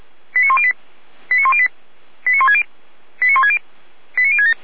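Five-tone selective call (ZVEI), heard as quick runs of five tones, tones of about 70 ms. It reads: 89108 89108 89160 89160 89067
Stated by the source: ZVEI five-tone selective call received by SDR.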